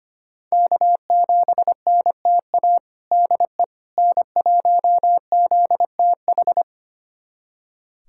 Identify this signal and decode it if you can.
Morse code, heard as a keyed tone, about 25 wpm, 700 Hz. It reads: K7NTA DE N1ZTH
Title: USA Callsign DE Callsign - 5 Characters - 25wpm